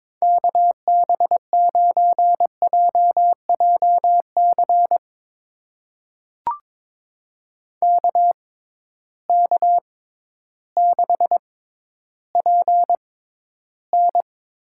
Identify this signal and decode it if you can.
Morse code, heard as a keyed tone, about 21 wpm, spaced as for 12 wpm, 700 Hz. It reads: KB9JJC KK6PN
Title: USA Call Signs - 6 Characters - 6x Character Spacing - 22wpm